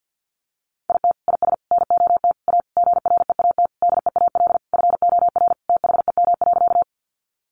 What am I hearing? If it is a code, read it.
Morse code, heard as a keyed tone, about 50 wpm, 700 Hz. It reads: IT IS NOT A GREAT DEAL FOR THEM.
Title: Sentences from Top 800 Words 50wpm